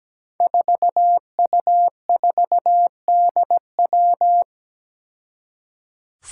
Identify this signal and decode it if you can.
Morse code, heard as a keyed tone, about 17 wpm, 700 Hz. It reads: 4U4DW
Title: Five Challenging Letter-Number Combinations 17wpm